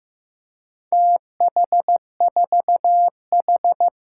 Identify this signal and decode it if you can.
Morse code, heard as a keyed tone, about 15 wpm, 700 Hz. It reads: TH4H
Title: Four Challenging Letter-Number Combinations 15wpm